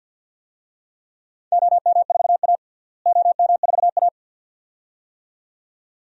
Morse code, heard as a keyed tone, about 52 wpm, 700 Hz. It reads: OM4A OM4A
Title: DXCC-100 Call Signs - 4 Characters - Repeated 2x - 50wpm